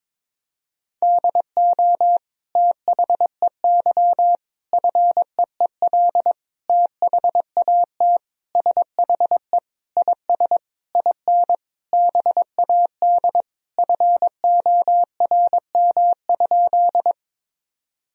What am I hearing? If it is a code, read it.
Morse code, heard as a keyed tone, about 22 wpm, 700 Hz. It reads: DO THEY FEEL THAT SHE IS IN BAD FORM?
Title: Sentences from Top 1000 Words - Review Part 1 - 22wpm